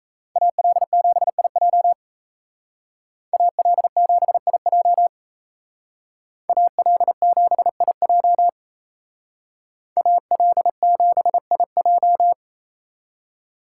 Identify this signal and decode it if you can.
Morse code, heard as a keyed tone, about 34 wpm, 700 Hz. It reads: AL7IJ AL7IJ AL7IJ AL7IJ
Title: Speed Racing - USA Call Signs 28wpm